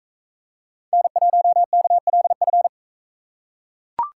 Morse code, heard as a keyed tone, about 44 wpm, 700 Hz. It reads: N1KLF